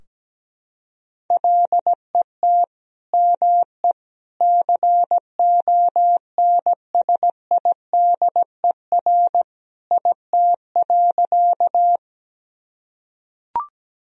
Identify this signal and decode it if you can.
Morse code, heard as a keyed tone, about 17 wpm, 700 Hz. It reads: LET ME CONSIDER IT.
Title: Sentences from Top 1000 Words - Review Part 4 - 17wpm